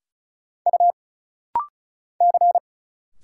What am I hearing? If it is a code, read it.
Morse code, heard as a keyed tone, about 34 wpm, 700 Hz. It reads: U C
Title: Single Letter-Number Alt - Rapid-Fire 35wpm